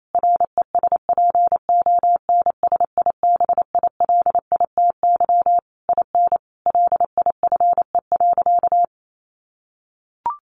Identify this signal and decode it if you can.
Morse code, heard as a keyed tone, about 28 wpm, 700 Hz. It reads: RESPONSIBILITY IN LIFE.
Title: Sentences from Top 1200 Words 28wpm